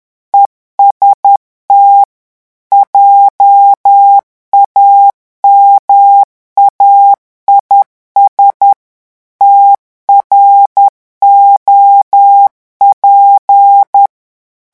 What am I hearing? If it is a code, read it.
Morse code, heard as a keyed tone, about 10 wpm, 800 Hz. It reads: EST JAMAIS TROP
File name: morse mp3 / revision francais 10 mots minute